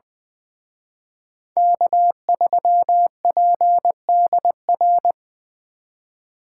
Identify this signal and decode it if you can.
Morse code, heard as a keyed tone, about 23 wpm, 700 Hz. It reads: K3PDR
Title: Speed Racing - USA 1x3 Call Signs 20wpm